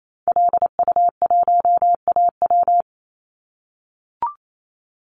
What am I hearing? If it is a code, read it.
Morse code, heard as a keyed tone, about 28 wpm, 700 Hz. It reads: LU1AW
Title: Call Signs - 28wpm